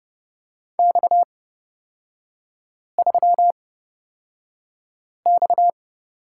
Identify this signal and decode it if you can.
Morse code, heard as a keyed tone, about 30 wpm, spaced as for 5 wpm, 700 Hz. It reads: X3X
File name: Lesson 37e - Letter x - Three Characters 30wpm